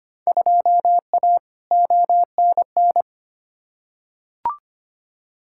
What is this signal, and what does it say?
Morse code, heard as a keyed tone, about 25 wpm, 700 Hz. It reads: 2A ONN
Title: Field Day Exchanges Encore - 25wpm